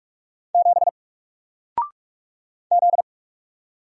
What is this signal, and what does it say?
Morse code, heard as a keyed tone, about 44 wpm, 700 Hz. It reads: 7 Z
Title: Single Letter-Number 45wpm